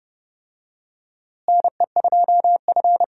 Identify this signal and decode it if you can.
Morse code, heard as a keyed tone, about 30 wpm, 700 Hz. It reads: NE2F
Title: USA Call Signs - 4 Characters - Repeated 3x - 30wpm